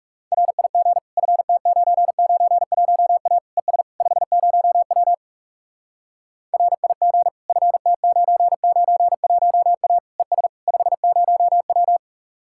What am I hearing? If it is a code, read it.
Morse code, heard as a keyed tone, about 40 wpm, 700 Hz. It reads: RIG FT991A ES 50W RIG FT991A ES 50W